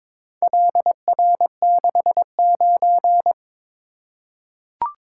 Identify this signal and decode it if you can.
Morse code, heard as a keyed tone, about 22 wpm, 700 Hz. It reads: LR69
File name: practice / 22wpm / Four Letter-Number Combinations - 22wpm